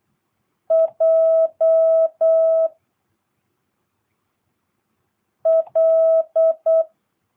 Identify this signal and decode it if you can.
Morse code, heard as a keyed tone, about 8 wpm, 650 Hz. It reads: J L